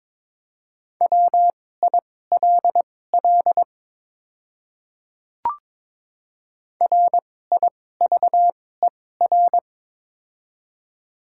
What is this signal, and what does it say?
Morse code, heard as a keyed tone, about 22 wpm, spaced as for 16 wpm, 700 Hz. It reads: WILL RIVER
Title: 2x Character Spacing - Top 300 Words - 22wpm